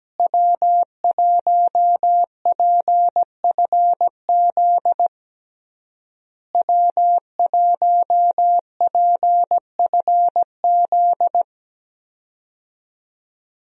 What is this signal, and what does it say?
Morse code, heard as a keyed tone, about 17 wpm, 700 Hz. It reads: W1PFZ W1PFZ
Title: USA Call Signs - 5 Characters - Repeated 3x - 17wpm